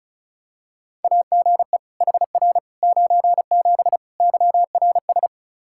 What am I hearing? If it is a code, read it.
Morse code, heard as a keyed tone, about 34 wpm, 700 Hz. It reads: AGE HR 97 YRS
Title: Practice QSO - Exchange 3 - More Info - 35wpm